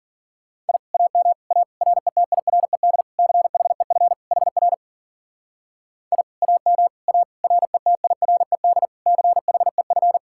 Morse code, heard as a keyed tone, about 52 wpm, 700 Hz. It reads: I AM A RETIRED CHEF HR I AM A RETIRED CHEF